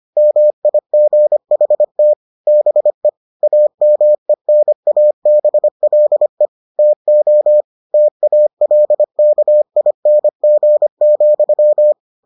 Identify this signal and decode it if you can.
Morse code, heard as a keyed tone, about 25 wpm, 600 Hz. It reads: MIGHT BE AMENABLE TO TALKING,